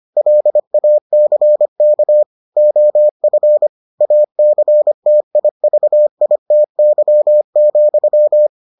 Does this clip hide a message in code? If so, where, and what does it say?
Yes: Morse code from 0.2 to 8.5 s, heard as a keyed tone, about 25 wpm, 600 Hz. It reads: LACK OF ACTIVITY,